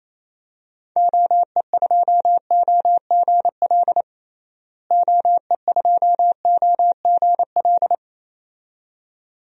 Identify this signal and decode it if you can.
Morse code, heard as a keyed tone, about 27 wpm, 700 Hz. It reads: OE2OGL OE2OGL
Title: DXCC-100 Call Signs - 6 Characters - Repeated 2x - 28wpm